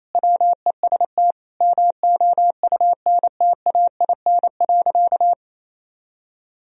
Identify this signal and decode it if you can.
Morse code, heard as a keyed tone, about 28 wpm, 700 Hz. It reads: WEST MOUNTAIN.
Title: Sentences from Top 1000 Words - Review Part 2 - 28wpm